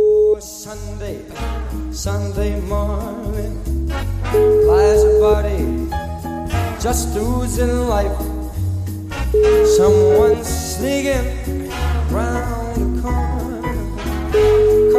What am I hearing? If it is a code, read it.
Ringback tone (Europe, 425 Hz), with noise.